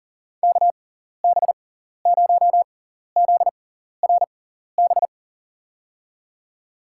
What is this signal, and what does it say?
Morse code, heard as a keyed tone, about 40 wpm, spaced as for 14 wpm, 700 Hz. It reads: KB0ZRB
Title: USA Call Signs - 6 Characters - 6x Character Spacing - 40wpm